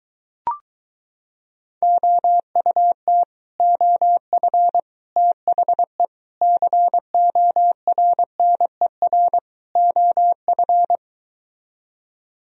Morse code, heard as a keyed tone, about 23 wpm, 700 Hz. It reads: OUT OF THE CORNER OF